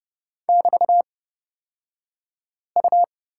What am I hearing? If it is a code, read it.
Morse code, heard as a keyed tone, about 30 wpm, spaced as for 5 wpm, 700 Hz. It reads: <BT>U
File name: Lesson 36d - BT - Two Characters 30wpm